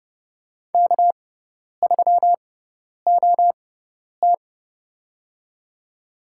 Morse code, heard as a keyed tone, about 31 wpm, spaced as for 10 wpm, 700 Hz. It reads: K3OT
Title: USA Call Signs - 4 Characters - 6x Character Spacing - 30wpm